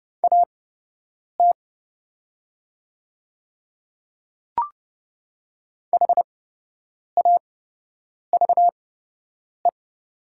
Morse code, heard as a keyed tone, about 30 wpm, spaced as for 8 wpm, 700 Hz. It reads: AT HAVE
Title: 8x Character Spacing - Top 200 Words - 30wpm